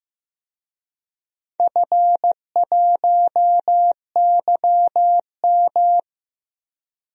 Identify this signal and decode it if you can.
Morse code, heard as a keyed tone, about 15 wpm, 700 Hz. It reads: F1YM
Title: DXCC-100 Call Signs - 4 Characters - Repeated 2x - 15wpm